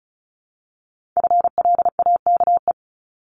Morse code, heard as a keyed tone, about 36 wpm, 700 Hz. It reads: FLAKE
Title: Multi-letter Phonemes Repeated 3x 35wpm